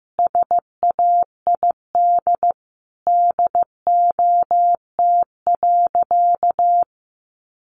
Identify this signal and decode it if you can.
Morse code, heard as a keyed tone, about 15 wpm, 700 Hz. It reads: SAID DOT.
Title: Sentences from Top 1500 Words 15wpm